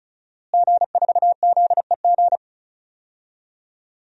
Morse code, heard as a keyed tone, about 36 wpm, 700 Hz. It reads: G4ZEG
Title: DXCC-100 Call Signs - 5 Characters - Repeated 3x - 35wpm